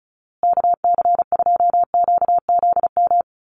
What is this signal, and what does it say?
Morse code, heard as a keyed tone, about 34 wpm, 700 Hz. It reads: KC2QZM
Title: USA Call Signs Repeated 2x 35wpm